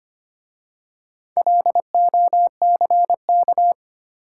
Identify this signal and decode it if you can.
Morse code, heard as a keyed tone, about 25 wpm, 690 Hz. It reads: LOCK
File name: practice / 25wpm / Easy Words with Hints - 25wpm